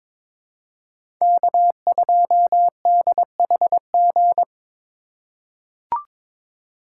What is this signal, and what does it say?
Morse code, heard as a keyed tone, about 22 wpm, 700 Hz. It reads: K2DHG
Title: USA Call Signs - 5 Characters - 6x Character Spacing - 22wpm